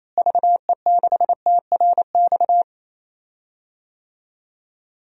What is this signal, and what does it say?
Morse code, heard as a keyed tone, about 28 wpm, 700 Hz. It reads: VE6TRX
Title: DXCC-100 Call Signs - 6 Characters - Repeated 2x - 28wpm